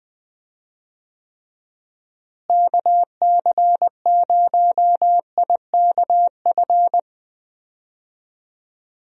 Morse code, heard as a keyed tone, about 20 wpm, 700 Hz. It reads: KC0IKF
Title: USA Call Signs - 6 Characters - Repeated 3x - 20wpm